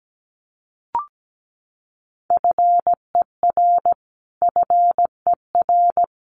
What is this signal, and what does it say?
Morse code, heard as a keyed tone, about 17 wpm, 700 Hz. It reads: FER FER
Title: Abbreviations Repeated 3x - 17wpm